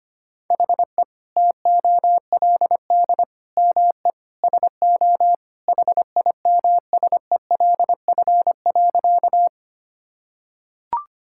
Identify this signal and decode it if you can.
Morse code, heard as a keyed tone, about 25 wpm, 700 Hz. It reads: HE TOLD ME SO HIMSELF.
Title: Sentences from Top 1600 Words 25wpm